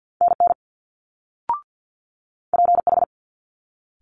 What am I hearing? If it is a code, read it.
Morse code, heard as a keyed tone, about 50 wpm, 700 Hz. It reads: NN PH
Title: Multi-letter Phonemes Quick Fire 50wpm